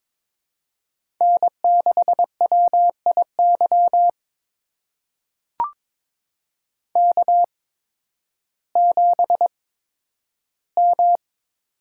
Morse code, heard as a keyed tone, about 22 wpm, spaced as for 12 wpm, 700 Hz. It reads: N6WIY K7M
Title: USA Call Signs - 5 Characters - 8x Character Spacing - 22wpm